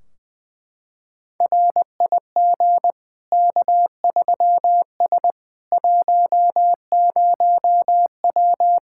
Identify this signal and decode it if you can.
Morse code, heard as a keyed tone, about 20 wpm, 700 Hz. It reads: RIG K3S 10W
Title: Practice QSO - Exchange 2 - Info - 20wpm